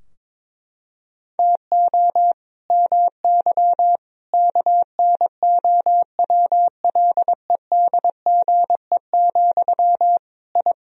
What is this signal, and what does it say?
Morse code, heard as a keyed tone, about 22 wpm, 700 Hz. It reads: TO MY KNOWLEDGE, I
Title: Sentences from Top 1000 Words - Review Part 3 - 22wpm